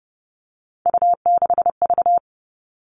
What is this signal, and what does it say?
Morse code, heard as a keyed tone, about 30 wpm, 700 Hz. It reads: U6V
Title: All Cut Numbers - 3-Digit 30wpm